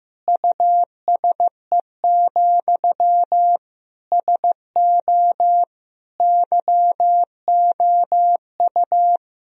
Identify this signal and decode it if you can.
Morse code, heard as a keyed tone, about 16 wpm, 700 Hz. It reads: USE, SO YOU
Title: Sentences from Top 1000 Words - Review Part 2 - 15wpm